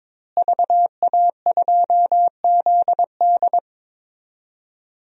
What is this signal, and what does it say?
Morse code, heard as a keyed tone, about 22 wpm, 700 Hz. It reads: VA2ZD
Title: DXCC-100 Call Signs - 5 Characters - Repeated 3x - 22wpm